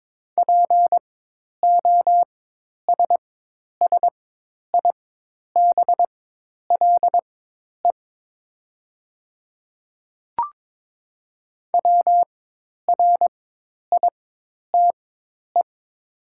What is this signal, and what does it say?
Morse code, heard as a keyed tone, about 22 wpm, spaced as for 10 wpm, 700 Hz. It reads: POSSIBLE WRITE
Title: 4x Character Spacing - Top 500 Words - 22wpm